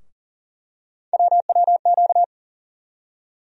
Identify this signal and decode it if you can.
Morse code, heard as a keyed tone, about 40 wpm, 700 Hz. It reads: WWQ
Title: USA Call Sign Postfixes - 3 Letters 40wpm